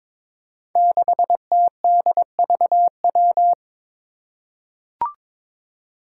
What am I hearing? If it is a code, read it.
Morse code, heard as a keyed tone, about 22 wpm, 700 Hz. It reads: 6TDVW
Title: Five Challenging Letter-Number Combinations 22wpm